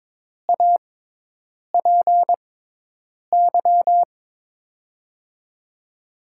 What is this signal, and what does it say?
Morse code, heard as a keyed tone, about 22 wpm, spaced as for 8 wpm, 700 Hz. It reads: APY
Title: USA Call Signs - 6 Characters - 6x Character Spacing - 22wpm